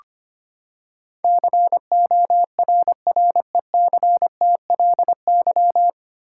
Morse code, heard as a keyed tone, about 25 wpm, 700 Hz. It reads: CORRECTLY
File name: Nine Letter Words - 25wpm